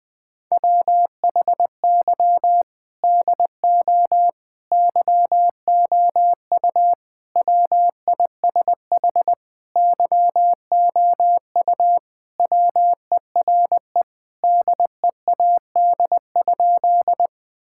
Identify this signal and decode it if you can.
Morse code, heard as a keyed tone, about 20 wpm, 700 Hz. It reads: WHY DO YOU WISH YOU WERE DEAD?
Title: Sentences from Top 1600 Words 20wpm